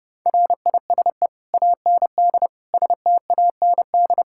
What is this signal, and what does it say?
Morse code, heard as a keyed tone, about 32 wpm, 700 Hz. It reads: RISE AND STAND
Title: Sentences from Top 1600 Words 30wpm